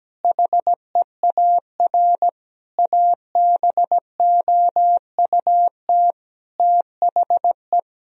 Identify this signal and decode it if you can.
Morse code, heard as a keyed tone, about 17 wpm, 700 Hz. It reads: HEAR ABOUT THE
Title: Sentences from Top 1000 Words - Review Part 4 - 17wpm